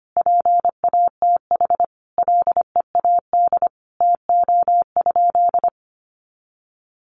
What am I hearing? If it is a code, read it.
Morse code, heard as a keyed tone, about 25 wpm, 700 Hz. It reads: PATH LEAD TO?
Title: Sentences from Top 1400 Words 25wpm